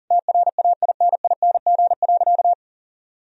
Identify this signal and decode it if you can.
Morse code, heard as a keyed tone, about 40 wpm, 700 Hz. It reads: TRAINING.
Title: Sentences from Top 1000 Words - Review Part 4 - 40wpm